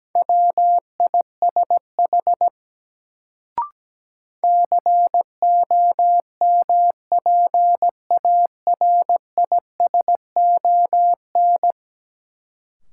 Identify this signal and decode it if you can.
Morse code, heard as a keyed tone, about 17 wpm, 700 Hz. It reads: WISH COMPARISON
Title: Sentences from Top 1600 Words - New Word Drill Alt - 17wpm